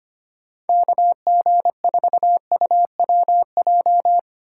Morse code, heard as a keyed tone, about 25 wpm, 700 Hz. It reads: KG4UWJ
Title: USA Call Signs - 6 Characters - 2x Character Spacing - 25wpm